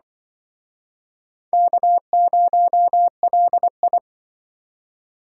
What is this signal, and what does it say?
Morse code, heard as a keyed tone, about 24 wpm, 700 Hz. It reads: K0LI
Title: Speed Racing - USA 1x2 Call Signs 20wpm